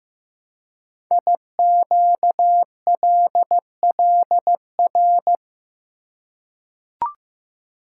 Morse code, heard as a keyed tone, about 15 wpm, 700 Hz. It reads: IQLLR
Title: Code Groups - 5 Letters - 15wpm